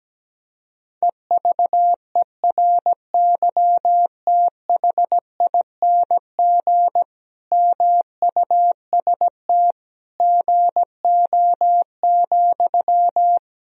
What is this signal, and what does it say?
Morse code, heard as a keyed tone, about 17 wpm, 700 Hz. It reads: EVERYTHING MUST GO,
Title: Sentences from Top 1000 Words - Review Part 4 - 17wpm